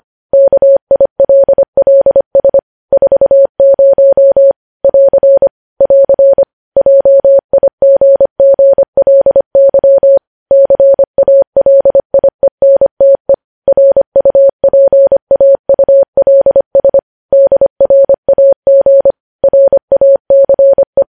Morse code, heard as a keyed tone, about 25 wpm, 550 Hz. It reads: KILLS 40 <AR> <AR> JIGGLY CALIENTE RUPAULS DRAG RACE
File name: podcast / News headlines-1x-25x25